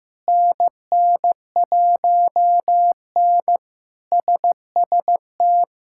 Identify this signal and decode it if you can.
Morse code, heard as a keyed tone, about 15 wpm, 700 Hz. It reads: NN1N SST
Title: K1USN SST QSO Practice - 15wpm